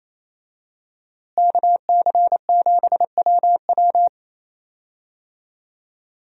Morse code, heard as a keyed tone, about 28 wpm, 700 Hz. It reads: KC7WW
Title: USA Call Signs - 5 Characters - Repeated 3x - 28wpm